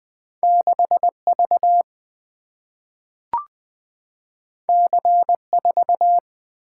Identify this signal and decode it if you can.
Morse code, heard as a keyed tone, about 20 wpm, 700 Hz. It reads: 6 V C 4
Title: Single-Letter Single-Number 20wpm